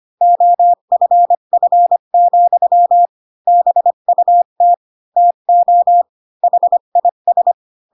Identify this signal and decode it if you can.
Morse code, heard as a keyed tone, about 25 wpm, 700 Hz. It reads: OFF, BUT TO HIS
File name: Pinocchio / Pinocchio ch5 20 25 ik1hge